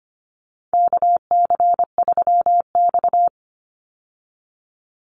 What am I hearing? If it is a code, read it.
Morse code, heard as a keyed tone, about 25 wpm, 700 Hz. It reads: KC3X
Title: Call Signs Repeated 3x - 25wpm